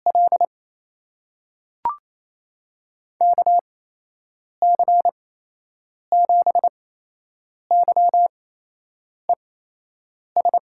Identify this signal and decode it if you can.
Morse code, heard as a keyed tone, about 28 wpm, spaced as for 8 wpm, 700 Hz. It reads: L KC7YES
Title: USA Call Signs - 8x Character Spacing - 28wpm